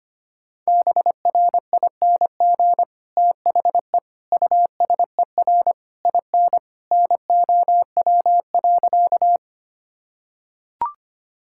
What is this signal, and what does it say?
Morse code, heard as a keyed tone, about 25 wpm, 700 Hz. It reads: BRING THE USER IN NOW.
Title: Sentences from Top 1000 Words - Review Part 5 - 25wpm